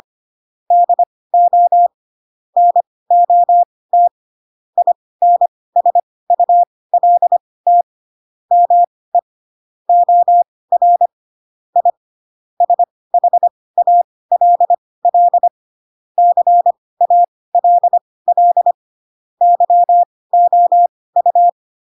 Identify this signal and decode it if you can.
Morse code, heard as a keyed tone, about 25 wpm, spaced as for 18 wpm, 700 Hz. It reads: DO NOT INSULT ME OR I SHALL CALL YOU
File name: Pinocchio / Pinocchio ch2 12 25 ik1hge